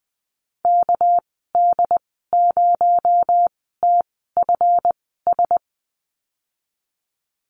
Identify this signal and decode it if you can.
Morse code, heard as a keyed tone, about 20 wpm, spaced as for 15 wpm, 700 Hz. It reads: KD0TFS
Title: 245 Lesson 39g - Letter z - Call Signs 20wpm